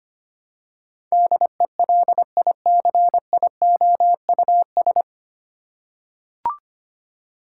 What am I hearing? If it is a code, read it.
Morse code, heard as a keyed tone, about 25 wpm, 700 Hz. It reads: DELICIOUS
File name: practice / 25wpm / 9 Letter Words - First 3 Letters Given - 25wpm